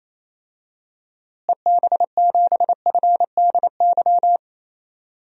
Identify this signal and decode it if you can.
Morse code, heard as a keyed tone, about 27 wpm, 700 Hz. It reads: EB7FDY
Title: DXCC-100 Call Signs - 6 Characters - Repeated 3x - 28wpm